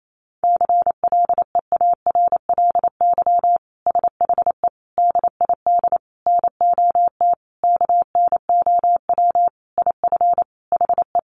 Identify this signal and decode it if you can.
Morse code, heard as a keyed tone, about 28 wpm, 700 Hz. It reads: CLEARLY SHE DID NOT KNOW IF HE
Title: Sentences from Top 1300 Words 28wpm